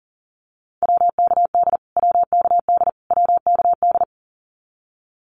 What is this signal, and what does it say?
Morse code, heard as a keyed tone, about 40 wpm, 700 Hz. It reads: WKD WKD WKD
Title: Abbreviations Repeated 3x - 40wpm